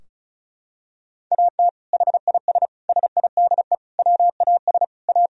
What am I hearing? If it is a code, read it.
Morse code, heard as a keyed tone, about 36 wpm, 700 Hz. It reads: AT HIS SIDE WAS A
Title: Sentences from Top 1000 Words - Review Part 2 - 35wpm